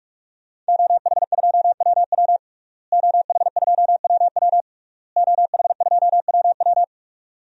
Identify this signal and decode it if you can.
Morse code, heard as a keyed tone, about 41 wpm, 700 Hz. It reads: OH2WW OH2WW OH2WW
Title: DXCC-100 Call Signs - 5 Characters - Repeated 3x - 45wpm